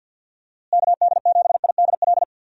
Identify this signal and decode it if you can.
Morse code, heard as a keyed tone, about 51 wpm, 700 Hz. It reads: KD7IDL
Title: USA Call Signs - 6 Characters - Repeated 2x - 50wpm